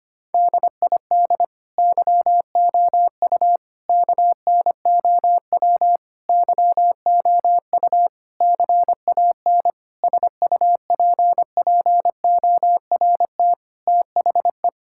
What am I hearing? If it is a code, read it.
Morse code, heard as a keyed tone, about 25 wpm, 700 Hz. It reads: DID YOU KNOW YOU CAN SUPPORT THE
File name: Sets of 9 Words - 25wpm